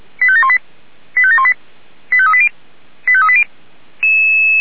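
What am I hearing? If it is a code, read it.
Five-tone selective call (ZVEI), heard as quick runs of five tones, tones of about 70 ms. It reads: 85718 85718 85380 85380
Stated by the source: ZVEI five-tone selective call received on an SDR.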